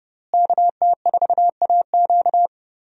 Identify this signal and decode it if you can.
Morse code, heard as a keyed tone, about 30 wpm, 700 Hz. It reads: KT4AQ
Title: USA Call Signs - 5 Characters - Repeated 3x - 30wpm